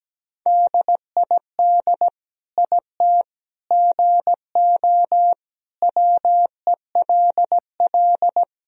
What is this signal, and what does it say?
Morse code, heard as a keyed tone, about 17 wpm, 700 Hz. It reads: DID IT GO WELL
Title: Sentences from Top 1000 Words - Review Part 1 - 17wpm